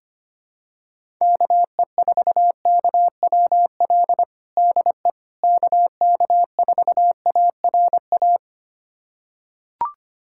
Morse code, heard as a keyed tone, about 25 wpm, 700 Hz. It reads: KE4KWL DE KK4ARA
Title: USA Callsign DE Callsign - 6 Characters - 25wpm